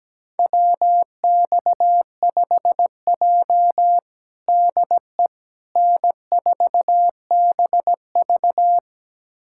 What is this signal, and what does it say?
Morse code, heard as a keyed tone, about 17 wpm, 700 Hz. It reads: WX5J DE N4BV
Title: USA Callsign DE Callsign - 4 Characters - 17wpm